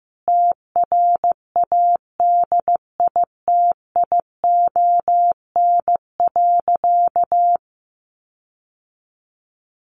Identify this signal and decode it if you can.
Morse code, heard as a keyed tone, about 15 wpm, 700 Hz. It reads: TRADITION.
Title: Sentences from Top 1500 Words 15wpm